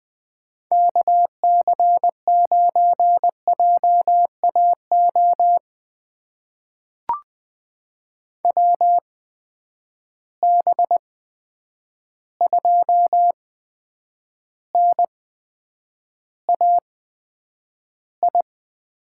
Morse code, heard as a keyed tone, about 21 wpm, spaced as for 9 wpm, 700 Hz. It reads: KC9JAO WB2NAI
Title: USA Call Signs - 6 Characters - 8x Character Spacing - 20wpm